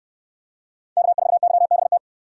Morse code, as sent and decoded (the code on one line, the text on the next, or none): -.- ....- -.-- -... -
K4YBT